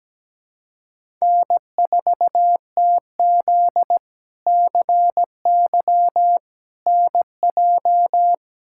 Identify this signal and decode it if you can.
Morse code, heard as a keyed tone, about 17 wpm, 700 Hz. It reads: N4TZ CY NJ
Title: K1USN SST QSO Practice - 17wpm